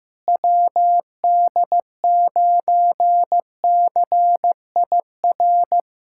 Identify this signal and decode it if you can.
Morse code, heard as a keyed tone, about 15 wpm, 700 Hz. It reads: WD9CIR
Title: K1USN SST QSO Practice - 15wpm